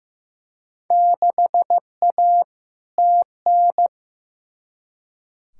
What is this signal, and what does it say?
Morse code, heard as a keyed tone, about 15 wpm, 700 Hz. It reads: 6A TN
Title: Field Day Exchanges Encore - 15wpm